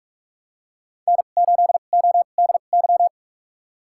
Morse code, heard as a keyed tone, about 46 wpm, spaced as for 32 wpm, 700 Hz. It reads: N8ODY
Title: USA Call Signs - 5 Characters - 2x Character Spacing - 45wpm